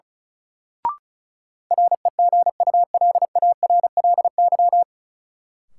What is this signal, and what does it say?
Morse code, heard as a keyed tone, about 36 wpm, 700 Hz. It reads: REGULARLY
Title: Sentences from Top 1400 Words - New Word Drill Alt - 35wpm